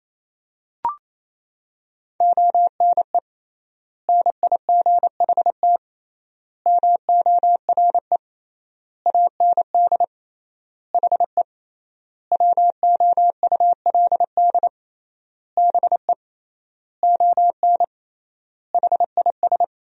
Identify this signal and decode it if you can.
Morse code, heard as a keyed tone, about 28 wpm, 700 Hz. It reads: ONE NIGHT MORE AND HE WOULD BE ON HIS
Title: 3x Word-Spacing - Sentences from Top 300 Words - 28wpm